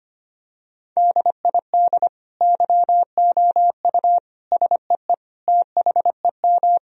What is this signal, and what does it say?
Morse code, heard as a keyed tone, about 25 wpm, 700 Hz. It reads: DID YOU SEE THEM
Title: Sentences from Top 1000 Words - Review Part 2 - 25wpm